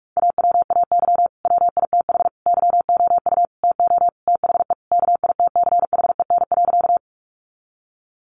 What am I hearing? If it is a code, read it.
Morse code, heard as a keyed tone, about 44 wpm, 700 Hz. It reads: AWAY WITH YOU TO THE KITCHEN.
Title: Sentences from Top 1400 Words 45wpm